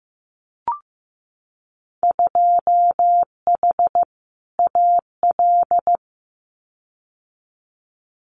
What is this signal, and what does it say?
Morse code, heard as a keyed tone, about 15 wpm, 700 Hz. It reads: 2H AL
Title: Winter Field Day Exchanges 15wpm